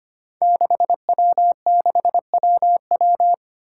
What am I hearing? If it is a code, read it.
Morse code, heard as a keyed tone, about 25 wpm, 700 Hz. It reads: 6W6WW
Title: Five Challenging Letter-Number Combinations 25wpm